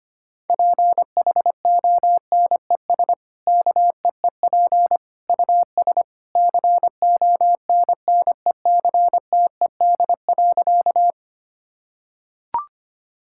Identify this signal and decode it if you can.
Morse code, heard as a keyed tone, about 25 wpm, 690 Hz. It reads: PHONES KEEP US CONNECTED.
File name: Easy Sentences with Hints - 25wpm